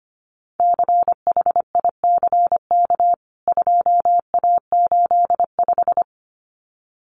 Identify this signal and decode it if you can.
Morse code, heard as a keyed tone, about 25 wpm, 700 Hz. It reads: CHICK 2A85
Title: CWT Exchange 25wpm